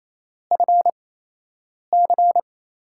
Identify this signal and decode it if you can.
Morse code, heard as a keyed tone, about 28 wpm, spaced as for 8 wpm, 700 Hz. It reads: FC
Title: USA Call Signs - 8x Character Spacing - 28wpm